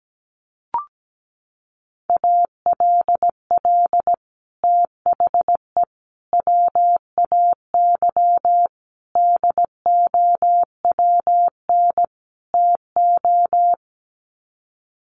Morse code, heard as a keyed tone, about 17 wpm, 700 Hz. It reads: ALL THE WAY DOWN TO
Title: Sets of 5 Words Encore2 17wpm